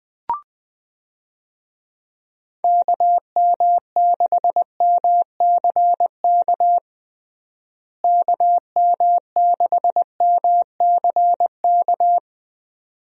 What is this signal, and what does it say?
Morse code, heard as a keyed tone, about 19 wpm, 700 Hz. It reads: KM6MCK KM6MCK
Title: USA Call Signs - 6 Characters - Repeated 2x - 20wpm